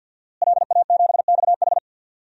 Morse code, heard as a keyed tone, about 51 wpm, 700 Hz. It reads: LA7XH
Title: DXCC-100 Call Signs - 5 Characters - Repeated 3x - 50wpm